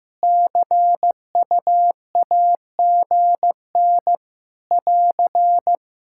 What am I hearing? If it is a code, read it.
Morse code, heard as a keyed tone, about 15 wpm, 700 Hz. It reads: CUAGN <AR>
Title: Practice QSO - Exchange 4 - Ending - 15wpm